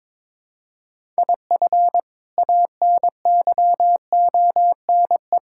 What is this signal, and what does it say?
Morse code, heard as a keyed tone, about 22 wpm, 700 Hz. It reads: IF ANYONE
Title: Sentences from Top 1000 Words - Review Part 6 - 22wpm